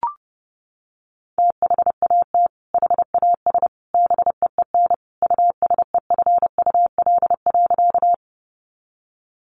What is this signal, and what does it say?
Morse code, heard as a keyed tone, about 30 wpm, 700 Hz. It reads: THAT HAS BEEN USEFUL.
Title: Sentences from Top 700 Words 30wpm